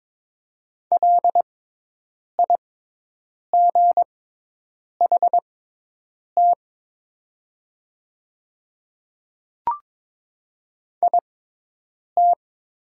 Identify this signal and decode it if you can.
Morse code, heard as a keyed tone, about 22 wpm, spaced as for 8 wpm, 700 Hz. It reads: LIGHT IT
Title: 6x Character Spacing - Top 500 Words - 22wpm